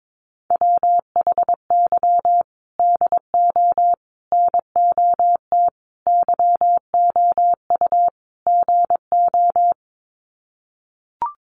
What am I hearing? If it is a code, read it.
Morse code, heard as a keyed tone, about 22 wpm, 700 Hz. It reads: WHY DO NOT YOU GO
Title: Sets of 5 Words Encore 22wpm